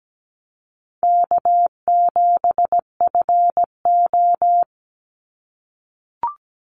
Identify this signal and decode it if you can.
Morse code, heard as a keyed tone, about 17 wpm, 700 Hz. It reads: K7FO
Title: NA SOTA Chaser Callsigns 17wpm